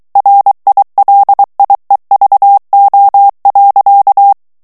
Morse code, heard as a keyed tone, about 23 wpm, 800 Hz. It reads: RILIEVO.
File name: ITALIEN texte clair 2 vitesse 22 mots minute